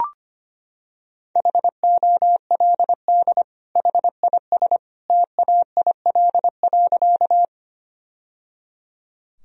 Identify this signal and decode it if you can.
Morse code, heard as a keyed tone, about 25 wpm, 700 Hz. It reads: HOLD HIS TAIL.
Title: Sentences from Top 1000 Words - Review Part 3 - 25wpm